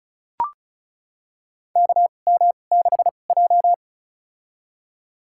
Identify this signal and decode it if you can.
Morse code, heard as a keyed tone, about 36 wpm, spaced as for 25 wpm, 700 Hz. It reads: KM6J
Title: USA Call Signs - 4 Characters - 2x Character Spacing - 35wpm